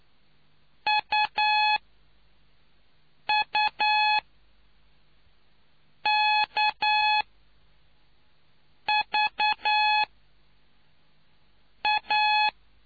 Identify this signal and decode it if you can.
Morse code, heard as a keyed tone, about 9 wpm, spaced as for 4 wpm, 850 Hz. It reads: UUKVA